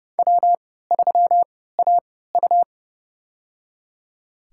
Morse code, heard as a keyed tone, about 30 wpm, spaced as for 17 wpm, 700 Hz. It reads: W3AU
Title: USA Call Signs - 3x Character Spacing - 30wpm